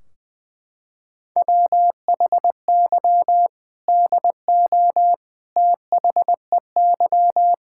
Morse code, heard as a keyed tone, about 20 wpm, 700 Hz. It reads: WHY DO THEY
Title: Sentences from Top 1600 Words 20wpm